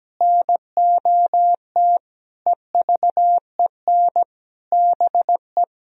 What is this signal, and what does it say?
Morse code, heard as a keyed tone, about 17 wpm, 700 Hz. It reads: NOT EVEN BE
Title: Sentences from Top 1000 Words - Review Part 3 - 17wpm